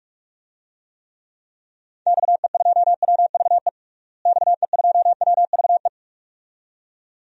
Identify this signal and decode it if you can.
Morse code, heard as a keyed tone, about 46 wpm, 700 Hz. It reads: XE2WVE XE2WVE
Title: DXCC-100 Call Signs - 6 Characters - Repeated 2x - 45wpm